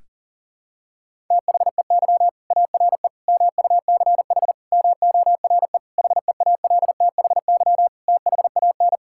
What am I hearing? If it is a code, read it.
Morse code, heard as a keyed tone, about 40 wpm, 700 Hz. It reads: THEY ARE MUCH MORE HEALTHY THAN
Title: Sentences from Top 1000 Words - Review Part 5 - 40wpm